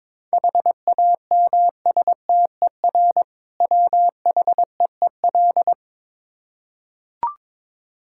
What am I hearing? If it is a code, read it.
Morse code, heard as a keyed tone, about 22 wpm, 690 Hz. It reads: HAMSTER WHEEL
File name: Difficult Words with Hints - 22wpm